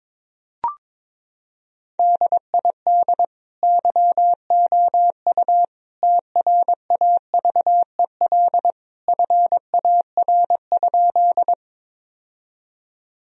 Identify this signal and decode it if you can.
Morse code, heard as a keyed tone, about 22 wpm, 700 Hz. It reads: DID YOU TRAVEL FAR?
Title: Sentences from Top 1000 Words - Review Part 2 - 22wpm